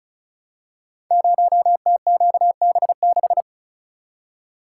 Morse code, heard as a keyed tone, about 36 wpm, 690 Hz. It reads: 0TQB6
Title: Code Groups - 5 Letter:Numbers - 35wpm